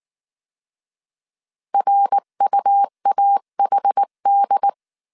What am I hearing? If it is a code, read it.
Morse code, heard as a keyed tone, about 19 wpm, 800 Hz. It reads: RUAHD
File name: CW / DET / 1 3 1500C S B 18wpm 1916